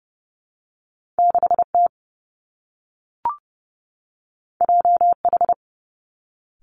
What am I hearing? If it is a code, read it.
Morse code, heard as a keyed tone, about 32 wpm, 700 Hz. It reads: 6 T J H
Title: Two Challenging Letter-Number Combinations 30wpm